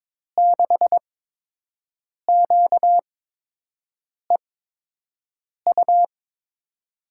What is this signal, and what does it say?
Morse code, heard as a keyed tone, about 21 wpm, spaced as for 6 wpm, 700 Hz. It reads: 6QEU
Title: USA Call Signs - 6 Characters - 8x Character Spacing - 22wpm